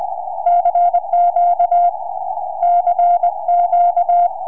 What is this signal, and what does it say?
Morse code, heard as a keyed tone, about 20 wpm, 690 Hz. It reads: CQ CQ
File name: ok7gu 4m